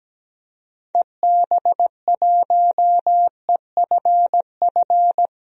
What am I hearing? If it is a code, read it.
Morse code, heard as a keyed tone, about 17 wpm, 700 Hz. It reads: EB1EFF
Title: DXCC-100 Call Signs - 6 Characters - Repeated 3x - 17wpm